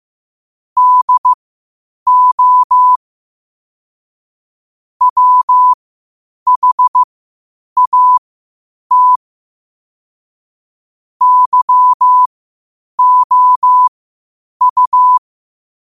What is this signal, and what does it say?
Morse code, heard as a keyed tone, about 15 wpm, spaced as for 8 wpm, 1000 Hz. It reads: DO WHAT YOU
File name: qotdfiles / 8wpm